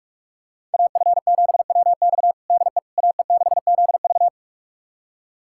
Morse code, heard as a keyed tone, about 46 wpm, 700 Hz. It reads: AF7WX DE AE6ZV
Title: USA Callsign DE Callsign - 5 Characters - 45wpm